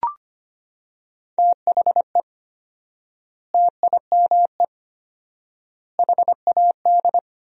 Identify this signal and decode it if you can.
Morse code, heard as a keyed tone, about 25 wpm, 700 Hz. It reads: THE TIME HAD
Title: 4x Word-Spacing - Sentences from Top 100 Words - 25wpm